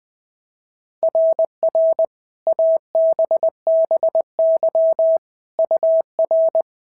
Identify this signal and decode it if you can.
Morse code, heard as a keyed tone, about 20 wpm, 690 Hz. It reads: RR ABBY UR
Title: SOTA QSO Practice 20wpm